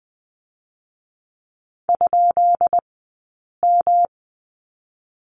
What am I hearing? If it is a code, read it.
Morse code, heard as a keyed tone, about 20 wpm, 700 Hz. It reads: ? M
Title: 217 Lesson 35d - Number 0 - Two Characters 20wpm